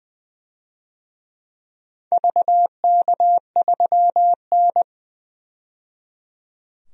Morse code, heard as a keyed tone, about 20 wpm, 700 Hz. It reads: VK3N
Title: DXCC-100 Call Signs - 4 Characters - 20wpm